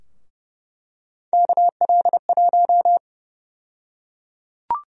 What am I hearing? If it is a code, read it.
Morse code, heard as a keyed tone, about 29 wpm, 700 Hz. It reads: KL1
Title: USA Call Sign Prefixes - Repeated 2x - 30wpm